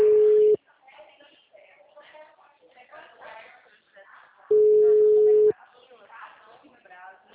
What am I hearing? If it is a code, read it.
Ringback tone (Europe, 425 Hz), with speech.